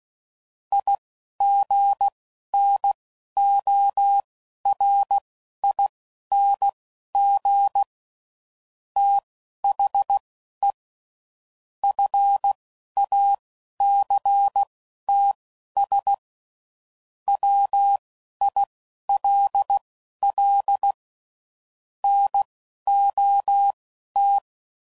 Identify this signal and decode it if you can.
Morse code, heard as a keyed tone, about 16 wpm, spaced as for 11 wpm, 800 Hz. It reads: IGNORING THE FACTS WILL NOT